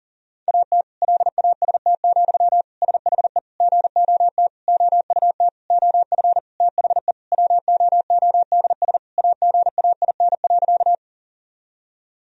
Morse code, heard as a keyed tone, about 40 wpm, 700 Hz. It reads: AT LAST, SHE GOT OUT OF THE WOODS AGAIN.
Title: Sentences from Top 1000 Words - Review Part 2 - 40wpm